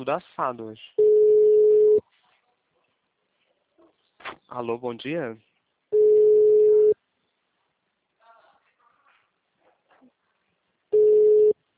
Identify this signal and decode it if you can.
Ringback tone (Europe, 425 Hz), with speech.
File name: out- -2011- - - .61